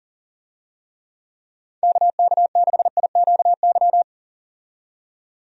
Morse code, heard as a keyed tone, about 40 wpm, 700 Hz. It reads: KK6IQY
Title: USA Call Signs - 6 Characters - 40wpm